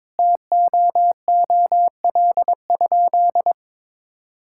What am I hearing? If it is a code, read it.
Morse code, heard as a keyed tone, about 22 wpm, 700 Hz. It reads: TOOL?
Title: Sentences from Top 1000 Words - Review Part 6 - 22wpm